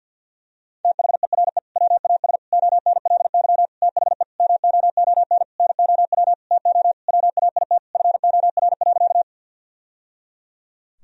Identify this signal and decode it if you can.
Morse code, heard as a keyed tone, about 50 wpm, 700 Hz. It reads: THERE WAS ONLY THE MOON NOW TO WAIT FOR.
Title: Sentences from Top 1000 Words - Review Part 3 - 50wpm